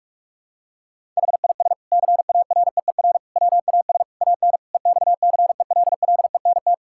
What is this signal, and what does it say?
Morse code, heard as a keyed tone, about 46 wpm, 700 Hz. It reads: HIS CAREER WAS AN EXCELLENT
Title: Sentences from Top 1000 Words - Review Part 6 - 45wpm